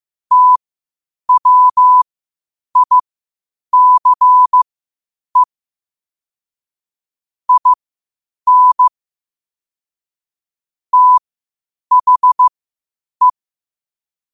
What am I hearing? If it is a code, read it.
Morse code, heard as a keyed tone, about 15 wpm, spaced as for 8 wpm, 1000 Hz. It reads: TWICE IN THE
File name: qotdfiles / 8wpm